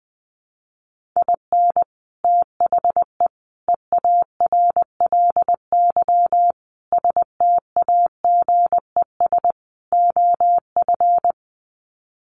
Morse code, heard as a keyed tone, about 19 wpm, 700 Hz. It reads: IN THE EARLY STAGES OF
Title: Sets of 5 Words Encore 20wpm